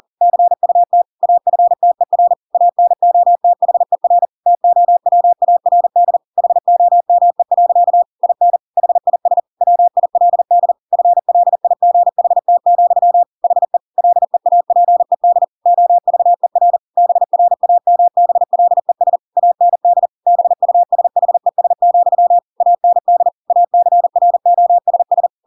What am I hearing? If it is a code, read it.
Morse code, heard as a keyed tone, about 40 wpm, 700 Hz. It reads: CUT AFTER ANOTHER TOWARD HOME. IN HIS WILD FLIGHT, HE LEAPED OVER BRAMBLES AND BUSHES, AND ACROSS
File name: Pinocchio / Pinocchio ch4 40 40 ik1hge